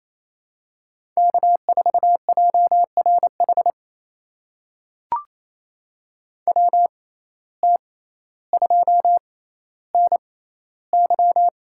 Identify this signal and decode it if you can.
Morse code, heard as a keyed tone, about 28 wpm, spaced as for 14 wpm, 700 Hz. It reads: K4JRH WT2NY
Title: USA Call Signs - 5 Characters - 6x Character Spacing - 28wpm